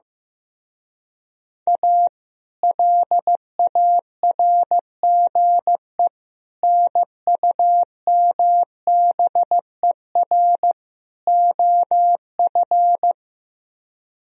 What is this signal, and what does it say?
Morse code, heard as a keyed tone, about 15 wpm, 700 Hz. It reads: A LARGE NUMBER OF
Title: Speed-Racing - Sets of 4 Words 15wpm